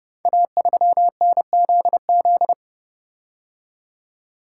Morse code, heard as a keyed tone, about 28 wpm, 700 Hz. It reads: A3NZZ
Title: Code Groups - 5 Letter:Numbers - 30wpm